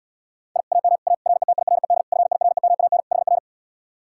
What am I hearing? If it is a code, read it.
Morse code, heard as a keyed tone, about 79 wpm, 700 Hz. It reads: I AM A RETIRED FIREMAN HR